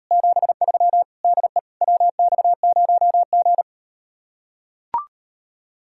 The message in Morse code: 73 DE WX0G

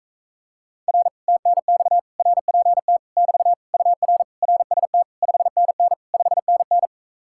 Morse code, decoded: R TNX RPT <BT> UR RST 5NN 5NN

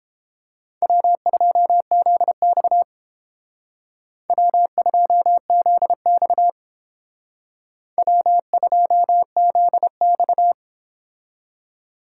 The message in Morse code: W2ZX W2ZX W2ZX

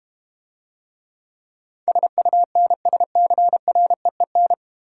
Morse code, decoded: SUNSCREEN

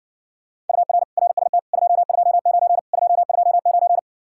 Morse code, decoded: UR RST 229 229

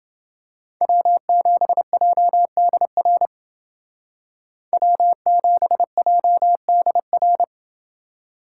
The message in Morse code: W7JDR W7JDR

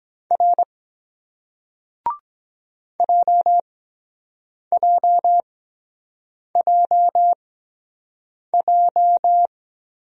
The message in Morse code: R JJJJ

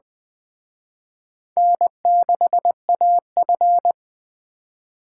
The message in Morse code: N6AF

